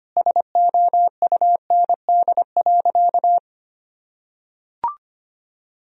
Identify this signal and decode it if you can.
Morse code, heard as a keyed tone, about 25 wpm, 700 Hz. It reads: SOUND.
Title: Sentences from Top 1000 Words - Review Part 1 - 25wpm